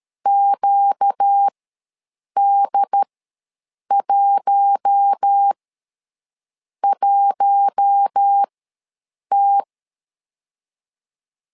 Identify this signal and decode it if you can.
Morse code, heard as a keyed tone, about 13 wpm, spaced as for 7 wpm, 800 Hz. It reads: QD11T
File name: LEZIONI / 4 Q1 ws 12wpm 2129